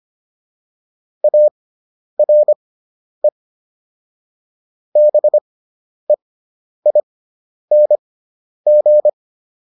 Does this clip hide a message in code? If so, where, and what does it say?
Yes: Morse code from 1.2 to 9.1 s, heard as a keyed tone, about 25 wpm, spaced as for 10 wpm, 600 Hz. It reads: ARE BEING